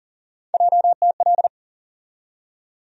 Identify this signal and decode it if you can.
Morse code, heard as a keyed tone, about 37 wpm, 700 Hz. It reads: JTL
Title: USA Call Sign Postfixes - 3 Letters 40wpm